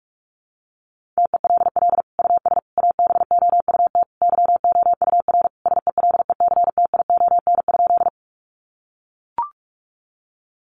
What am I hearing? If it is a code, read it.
Morse code, heard as a keyed tone, about 44 wpm, 700 Hz. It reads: TELL US ABOUT YOUR SELECTION?
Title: Sentences from Top 1400 Words 45wpm